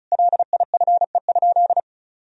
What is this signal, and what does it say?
Morse code, heard as a keyed tone, about 36 wpm, 700 Hz. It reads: LIFE?
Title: Sentences from Top 300 Words and Abbreviations 35wpm